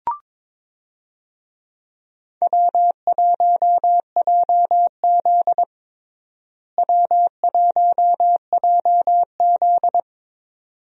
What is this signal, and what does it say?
Morse code, heard as a keyed tone, about 22 wpm, 700 Hz. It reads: W1JZ W1JZ
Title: USA Call Signs - 4 Characters - Repeated 3x - 22wpm